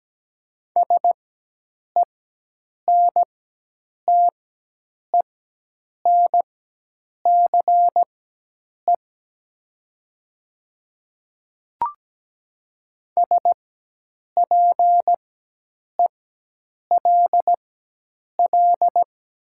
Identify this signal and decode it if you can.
Morse code, heard as a keyed tone, about 17 wpm, spaced as for 8 wpm, 700 Hz. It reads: SENTENCE SPELL
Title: 4x Character Spacing - Top 500 Words - 17wpm